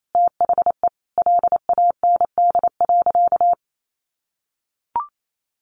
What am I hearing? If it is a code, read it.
Morse code, heard as a keyed tone, about 28 wpm, 700 Hz. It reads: THE LAND.
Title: Sentences from Top 1500 Words 28wpm